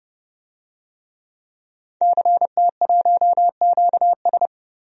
Morse code, heard as a keyed tone, about 30 wpm, 700 Hz. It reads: CT1QS
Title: DXCC-100 Call Signs - 5 Characters - 30wpm